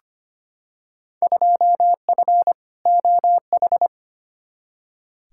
Morse code, heard as a keyed tone, about 25 wpm, 700 Hz. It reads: 2F OH
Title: Field Day Exchanges - 25wpm